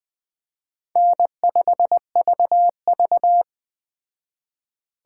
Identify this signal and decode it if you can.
Morse code, heard as a keyed tone, about 20 wpm, 700 Hz. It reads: N5VV
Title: Speed Racing - USA 1x2 Call Signs 17wpm